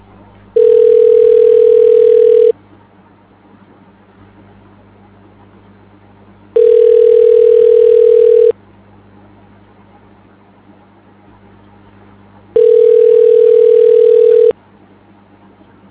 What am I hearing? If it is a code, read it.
Ringback tone (North America), 440 + 480 Hz.